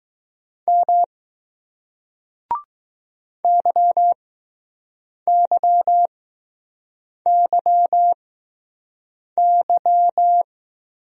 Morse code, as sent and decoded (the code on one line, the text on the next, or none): -- / -.-- -.-- -.-- -.--
M YYYY